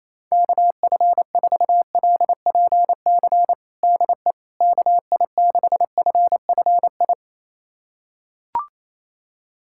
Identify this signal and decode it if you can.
Morse code, heard as a keyed tone, about 27 wpm, 700 Hz. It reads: KF4LPC DE KI6FFI
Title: USA Callsign DE Callsign - 6 Characters - 28wpm